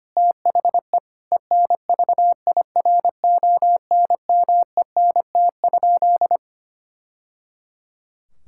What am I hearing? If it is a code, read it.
Morse code, heard as a keyed tone, about 25 wpm, 700 Hz. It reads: THE ENVIRONMENT?
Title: Sentences from Top 1000 Words - Review Part 6 - 25wpm